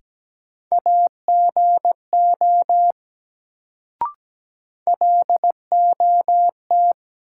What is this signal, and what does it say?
Morse code, heard as a keyed tone, about 17 wpm, 700 Hz. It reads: AGO LOT
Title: Sentences from Top 500 Words - New Word Drill - 17wpm